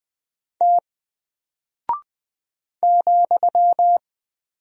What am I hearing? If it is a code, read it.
Morse code, heard as a keyed tone, about 20 wpm, 700 Hz. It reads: T ,
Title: Single Character (Letter:Number:Punctuation:Prosign) 20wpm